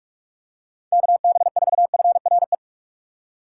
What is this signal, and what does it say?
Morse code, heard as a keyed tone, about 46 wpm, 700 Hz. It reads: KB4FRE
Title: USA Call Signs - 6 Characters - 4x Character Spacing - 45wpm